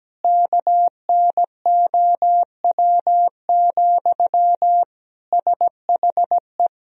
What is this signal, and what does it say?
Morse code, heard as a keyed tone, about 17 wpm, 700 Hz. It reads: KNOW, SHE
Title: Sentences from Top 1000 Words - Review Part 3 - 17wpm